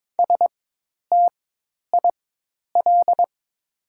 Morse code, heard as a keyed tone, about 23 wpm, spaced as for 10 wpm, 700 Hz. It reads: STIL
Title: 4x Character Spacing - Top 300 Words - 22wpm